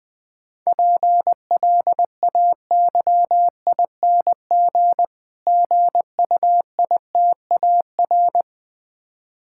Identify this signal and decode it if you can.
Morse code, heard as a keyed tone, about 20 wpm, 700 Hz. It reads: PLAYING GUITAR
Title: Easy Words - Speed-Racing with Hints - 20wpm